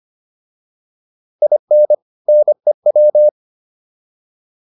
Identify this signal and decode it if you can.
Morse code, heard as a keyed tone, about 25 wpm, 600 Hz. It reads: IN NEW